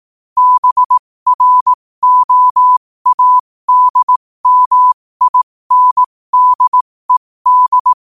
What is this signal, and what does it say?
Morse code, heard as a keyed tone, about 18 wpm, spaced as for 15 wpm, 1000 Hz. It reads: BROADMINDED
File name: qotdfiles / 15wpm